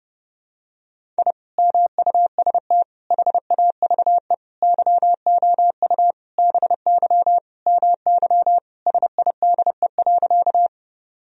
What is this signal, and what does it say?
Morse code, heard as a keyed tone, about 30 wpm, 700 Hz. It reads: I MUST HAVE YOU BY MY SIDE.
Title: Sentences from Top 1000 Words - Review Part 1 - 30wpm